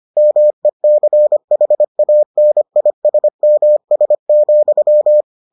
Morse code, heard as a keyed tone, about 25 wpm, 600 Hz. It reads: MECHANISMS,